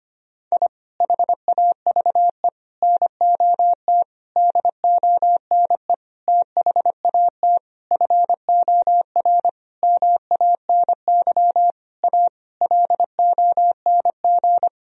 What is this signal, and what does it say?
Morse code, heard as a keyed tone, about 25 wpm, 700 Hz. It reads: I HAVE NOT DONE THAT FOR MANY A LONG
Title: Sentences from Top 1000 Words - Review Part 1 - 25wpm